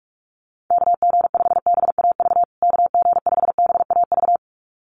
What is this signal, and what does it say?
Morse code, heard as a keyed tone, about 44 wpm, 700 Hz. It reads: KG5BAV KG5BAV
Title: USA Call Signs Repeated 2x 45wpm